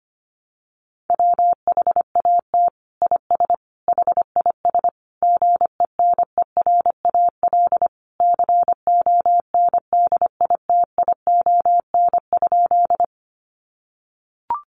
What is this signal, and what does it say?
Morse code, heard as a keyed tone, about 25 wpm, 700 Hz. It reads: WHAT IS HIS GENERAL CONDITION?
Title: Sentences from Top 1000 Words 25wpm